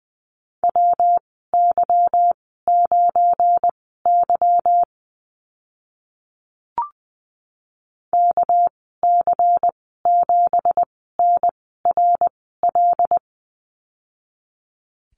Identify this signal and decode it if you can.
Morse code, heard as a keyed tone, about 20 wpm, spaced as for 15 wpm, 700 Hz. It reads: WY9Y KC7NRL
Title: 200 Lesson 32g - Letter k - Call Signs 20wpm